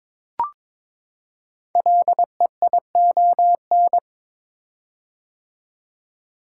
Morse code, heard as a keyed tone, about 22 wpm, 700 Hz. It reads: LEION
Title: Code Groups - 5 Letter:Numbers - 22wpm